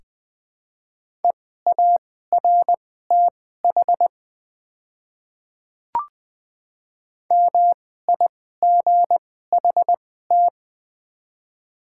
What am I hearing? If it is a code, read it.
Morse code, heard as a keyed tone, about 20 wpm, spaced as for 14 wpm, 700 Hz. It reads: EARTH MIGHT
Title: 2x Character Spacing - Top 300 Words - 20wpm